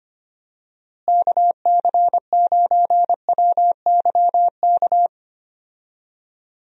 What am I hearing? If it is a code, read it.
Morse code, heard as a keyed tone, about 25 wpm, 700 Hz. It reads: KC9WYK
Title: USA Call Signs - 6 Characters - Repeated 2x - 25wpm